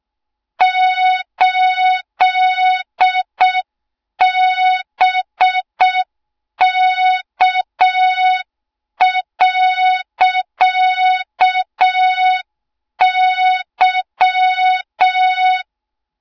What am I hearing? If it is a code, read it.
Morse code, heard as a keyed tone, about 6 wpm, 3700 Hz. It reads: 8BK.Y
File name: Morze-HA2MN / hang / 20 GYAKORLAT